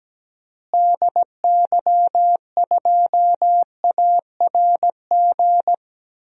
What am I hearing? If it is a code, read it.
Morse code, heard as a keyed tone, about 17 wpm, 700 Hz. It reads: DY2ARG
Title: DXCC-100 Call Signs - 6 Characters - Repeated 3x - 17wpm